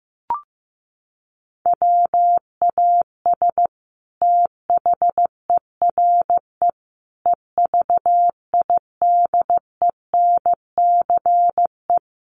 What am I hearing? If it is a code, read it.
Morse code, heard as a keyed tone, about 15 wpm, 700 Hz. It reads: WAS THERE EVIDENCE
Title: Sentences from Top 1300 Words 15wpm